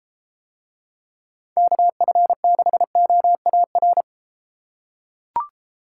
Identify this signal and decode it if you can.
Morse code, heard as a keyed tone, about 31 wpm, 700 Hz. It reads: KF6OAR